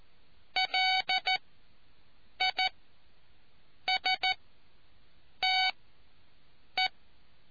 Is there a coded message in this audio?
Yes: Morse code from 0.6 to 6.9 s, heard as a keyed tone, about 14 wpm, spaced as for 6 wpm, 750 Hz. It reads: LISTE